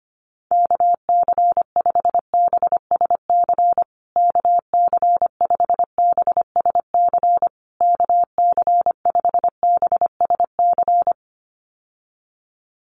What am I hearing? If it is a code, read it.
Morse code, heard as a keyed tone, about 25 wpm, 700 Hz. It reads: KC5BSC KC5BSC KC5BSC
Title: USA Call Signs Repeated 3x 25wpm